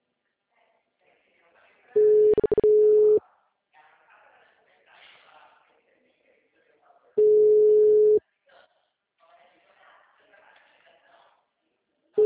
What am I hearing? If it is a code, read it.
Ringback tone (Europe, 425 Hz), with speech.